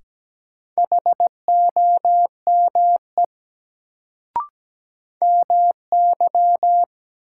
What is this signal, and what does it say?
Morse code, heard as a keyed tone, about 17 wpm, 700 Hz. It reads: HOME MY
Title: Sentences from Top 200 Words - New Word Drill - 17wpm